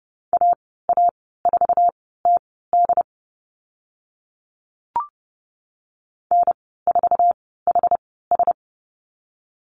Morse code, heard as a keyed tone, about 30 wpm, spaced as for 17 wpm, 700 Hz. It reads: AA4TD N4HS
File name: Lesson 14g - Number 2 - Call Signs 30wpm